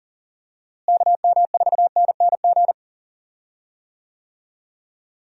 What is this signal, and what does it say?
Morse code, heard as a keyed tone, about 40 wpm, 700 Hz. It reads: KM4NNG
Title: USA Call Signs - 6 Characters - Repeated 2x - 40wpm